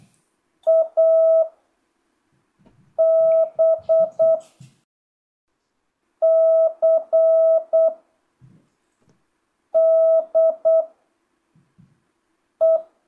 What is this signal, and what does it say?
Morse code, heard as a keyed tone, about 8 wpm, 650 Hz. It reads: A B C D E
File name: jh8cbh / 2025cw